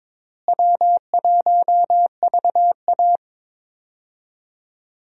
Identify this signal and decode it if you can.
Morse code, heard as a keyed tone, about 21 wpm, 700 Hz. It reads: W1VA
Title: USA Call Signs - 4 Characters - Repeated 3x - 22wpm